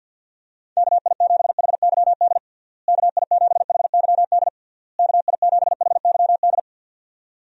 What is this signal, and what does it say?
Morse code, heard as a keyed tone, about 49 wpm, 700 Hz. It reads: KI7HYD KI7HYD KI7HYD
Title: USA Call Signs - 6 Characters - Repeated 3x - 50wpm